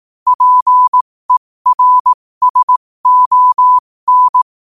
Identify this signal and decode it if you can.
Morse code, heard as a keyed tone, about 18 wpm, spaced as for 15 wpm, 1000 Hz. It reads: PERSON
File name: qotdfiles / 15wpm